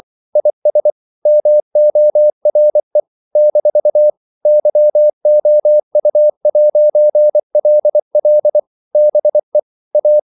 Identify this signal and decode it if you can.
Morse code, heard as a keyed tone, about 24 wpm, 600 Hz. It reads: IS MORE - YOU'LL BE A
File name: CW / Practice / If24